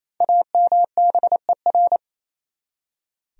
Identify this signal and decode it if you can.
Morse code, heard as a keyed tone, about 28 wpm, 700 Hz. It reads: AMBER
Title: USA Common Names - Max 5 Letters - 28wpm